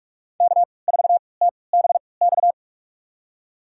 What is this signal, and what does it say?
Morse code, heard as a keyed tone, about 45 wpm, spaced as for 25 wpm, 700 Hz. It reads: K4TBX
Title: USA Call Signs - 5 Characters - 3x Character Spacing - 45wpm